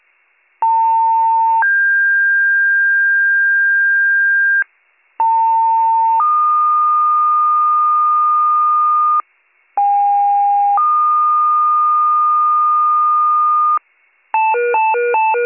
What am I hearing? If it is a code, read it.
Two-tone page, heard as a one-second tone, then a three-second tone, about 900 then 1600 Hz, about 900 then 1220 Hz, about 810 then 1220 Hz. It is heard from 0.6 to 13.8 s.